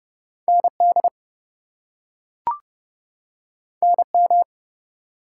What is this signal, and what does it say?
Morse code, heard as a keyed tone, about 30 wpm, 700 Hz. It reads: ND NM
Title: US States and Canadian Provinces 30wpm